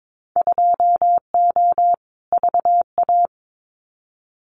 Morse code, heard as a keyed tone, about 22 wpm, 700 Hz. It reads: 2O VA
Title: Winter Field Day Exchanges 22wpm